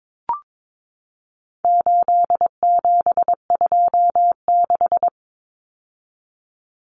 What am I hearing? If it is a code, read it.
Morse code, heard as a keyed tone, about 22 wpm, 700 Hz. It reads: 8726
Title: Common Cut Numbers - 4-Digit 22wpm